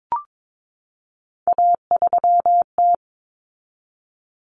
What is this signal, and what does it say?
Morse code, heard as a keyed tone, about 22 wpm, 700 Hz. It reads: A3T
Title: Common Cut Numbers - 3-Digit 22wpm